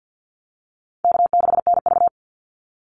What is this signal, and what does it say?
Morse code, heard as a keyed tone, about 50 wpm, 700 Hz. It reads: K6NV
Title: Easy Call Signs Repeated 3x - 50wpm